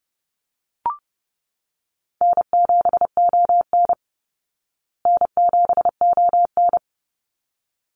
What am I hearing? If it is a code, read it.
Morse code, heard as a keyed tone, about 30 wpm, 700 Hz. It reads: N7ON N7ON
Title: Call Signs Repeated 3x - 30wpm